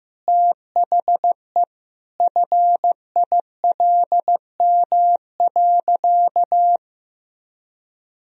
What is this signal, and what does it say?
Morse code, heard as a keyed tone, about 15 wpm, 700 Hz. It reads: THE FILM.
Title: Sentences from Top 1000 Words - Review Part 3 - 15wpm